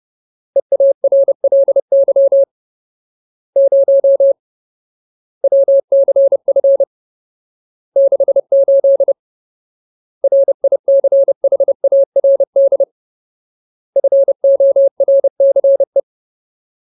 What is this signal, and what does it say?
Morse code, heard as a keyed tone, about 30 wpm, 550 Hz. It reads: EARLY 0 WCF 68 RICHARD FORCE
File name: Ultimate Word List / 30WPM Ultimate